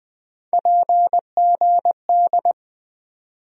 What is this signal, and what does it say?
Morse code, heard as a keyed tone, about 20 wpm, 700 Hz. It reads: PGD
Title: USA Call Sign Postfixes - 3 Letters 20wpm